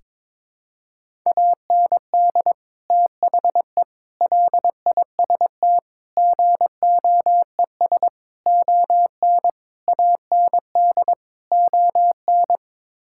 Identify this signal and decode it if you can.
Morse code, heard as a keyed tone, about 22 wpm, 700 Hz. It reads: AND THE LIST GOES ON AND ON
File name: practice / 22wpm / Sets of 7 Words - 22wpm